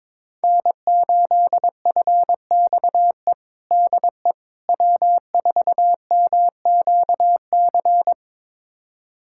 Morse code, heard as a keyed tone, about 21 wpm, 700 Hz. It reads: N8FXE DE W4MQC
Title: USA Callsign DE Callsign 22wpm